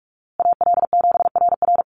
Morse code, heard as a keyed tone, about 44 wpm, 700 Hz. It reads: AL7RR
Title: USA Call Signs Repeated 3x 45wpm